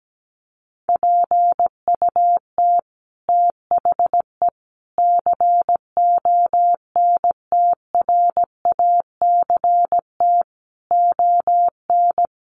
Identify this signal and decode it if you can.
Morse code, heard as a keyed tone, about 17 wpm, 700 Hz. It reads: PUT THE CONTRACT ON
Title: Sentences from Top 1400 Words 17wpm